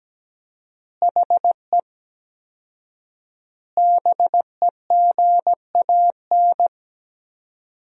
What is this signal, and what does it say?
Morse code, heard as a keyed tone, about 17 wpm, 700 Hz. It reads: HE BEGAN
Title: 4x Word-Spacing - Sentences from Top 300 Words - 17wpm